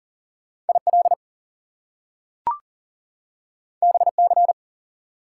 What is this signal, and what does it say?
Morse code, heard as a keyed tone, about 40 wpm, 700 Hz. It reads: IL BC
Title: US States and Canadian Provinces 40wpm